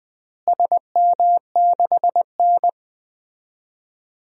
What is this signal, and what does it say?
Morse code, heard as a keyed tone, about 20 wpm, 700 Hz. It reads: SM6N
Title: DXCC-100 Call Signs - 4 Characters - Repeated 2x - 20wpm